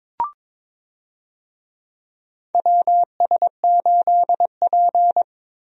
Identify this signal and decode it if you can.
Morse code, heard as a keyed tone, about 22 wpm, 700 Hz. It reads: WS8P
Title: USA Call Signs - 4 Characters - Repeated 2x - 22wpm